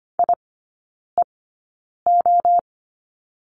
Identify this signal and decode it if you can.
Morse code, heard as a keyed tone, about 25 wpm, spaced as for 9 wpm, 700 Hz. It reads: IEO
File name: Lesson 17e - Letter u - Three Characters 25wpm